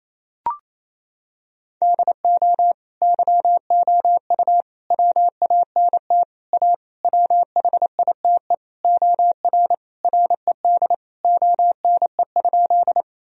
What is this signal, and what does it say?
Morse code, heard as a keyed tone, about 28 wpm, 700 Hz. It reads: DO YOU WANT A WHITE OR RED ONE?
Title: Sentences from Top 1000 Words - Review Part 2 - 28wpm